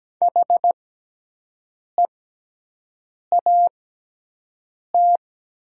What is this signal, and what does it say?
Morse code, heard as a keyed tone, about 17 wpm, spaced as for 6 wpm, 700 Hz. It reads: HEAT
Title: 6x Character Spacing - Top 500 Words - 17wpm